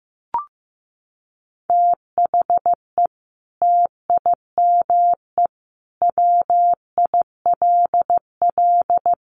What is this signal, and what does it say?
Morse code, heard as a keyed tone, about 15 wpm, 700 Hz. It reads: THE TIME WILL
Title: Sentences from Top 100 Words 15wpm